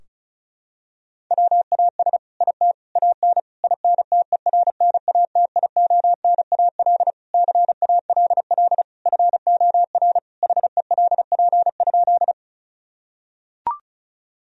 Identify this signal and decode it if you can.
Morse code, heard as a keyed tone, about 36 wpm, 700 Hz. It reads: WAS IT AN INTERNATIONAL CALL FOR HELP?
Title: Sentences from Top 1000 Words - Review Part 4 - 35wpm